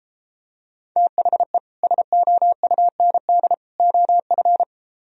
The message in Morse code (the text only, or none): THE SOUND OF